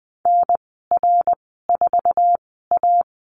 Morse code, heard as a keyed tone, about 20 wpm, spaced as for 14 wpm, 700 Hz. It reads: NR4A
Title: 055 Lesson 10g - Letter r - Call Signs 20wpm